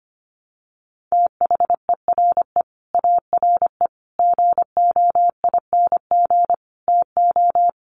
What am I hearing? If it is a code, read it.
Morse code, heard as a keyed tone, about 24 wpm, 700 Hz. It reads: THERE ARE GOING TO
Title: Sets of 5 Words 25wpm